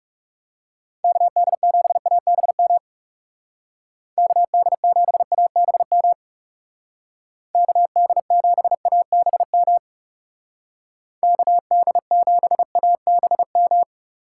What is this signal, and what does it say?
Morse code, heard as a keyed tone, about 38 wpm, 700 Hz. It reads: KD7ABM KD7ABM KD7ABM KD7ABM